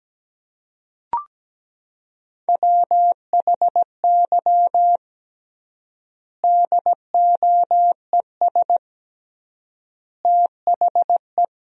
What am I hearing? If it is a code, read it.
Morse code, heard as a keyed tone, about 17 wpm, 700 Hz. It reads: WHY DOES THE
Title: 3x Word-Spacing - Sentences from Top 500 Words - 17wpm